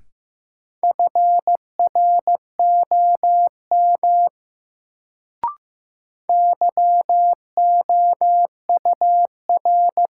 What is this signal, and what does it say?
Morse code, heard as a keyed tone, about 15 wpm, 700 Hz. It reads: FROM YOUR
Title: Sentences from Top 100 Words - New Word Drill - 15wpm